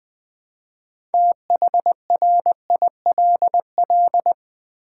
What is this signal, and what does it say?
Morse code, heard as a keyed tone, about 20 wpm, 700 Hz. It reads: THRILL
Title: Build A Word - 2 Letters At A Time - 8-Letter Words - 20wpm